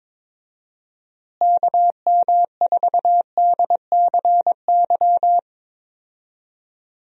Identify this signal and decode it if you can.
Morse code, heard as a keyed tone, about 22 wpm, 700 Hz. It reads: KM4DCY
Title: USA Call Signs - 6 Characters - Repeated 2x - 22wpm